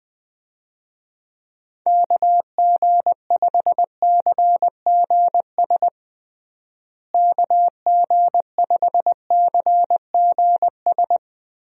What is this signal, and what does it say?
Morse code, heard as a keyed tone, about 20 wpm, 700 Hz. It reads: KG5CGS KG5CGS
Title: USA Call Signs - 6 Characters - Repeated 3x - 20wpm